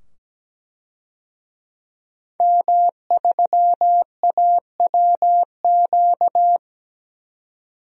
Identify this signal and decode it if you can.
Morse code, heard as a keyed tone, about 17 wpm, 700 Hz. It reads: M3AWQ
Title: DXCC-100 Call Signs - 5 Characters - 17wpm